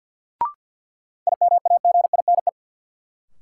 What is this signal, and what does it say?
Morse code, heard as a keyed tone, about 51 wpm, 700 Hz. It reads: IMAGINE
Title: Sentences from Top 1400 Words - New Word Drill - 50wpm